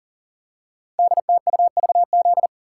Morse code, heard as a keyed tone, about 38 wpm, 690 Hz. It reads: DTUVZ
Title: Code Groups - 5 Letters - 40wpm